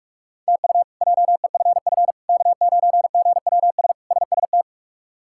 Morse code, heard as a keyed tone, about 45 wpm, 700 Hz. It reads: TU JEFF K9GWS SST
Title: K1USN SST QSO Practice - 45wpm